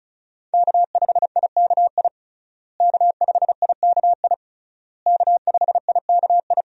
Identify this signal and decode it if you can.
Morse code, heard as a keyed tone, about 36 wpm, 700 Hz. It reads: K5IKI K5IKI K5IKI
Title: USA Call Signs - 5 Characters - Repeated 3x - 35wpm